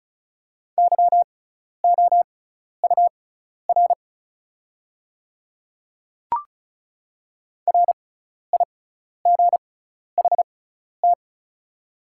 Morse code, heard as a keyed tone, about 36 wpm, spaced as for 12 wpm, 700 Hz. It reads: YOUR RIGHT
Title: 6x Character Spacing - Top 300 Words - 35wpm